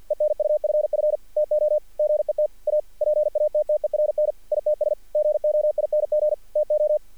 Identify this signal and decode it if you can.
Morse code, heard as a keyed tone, about 53 wpm, 620 Hz. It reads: ENUFF TO GET A PATTERN ITS GOING TO
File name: archived / CRSnet